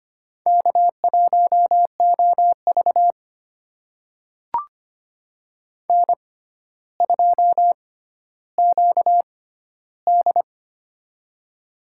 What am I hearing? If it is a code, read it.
Morse code, heard as a keyed tone, about 25 wpm, spaced as for 13 wpm, 700 Hz. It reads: K1OV N2QD
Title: USA Call Signs - 4 Characters - 6x Character Spacing - 25wpm